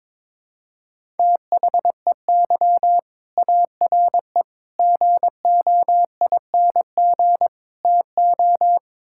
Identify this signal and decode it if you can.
Morse code, heard as a keyed tone, about 21 wpm, 700 Hz. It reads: THEY ARE GOING TO
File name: practice / 22wpm / Sentences with 4-Word Phrases - 22wpm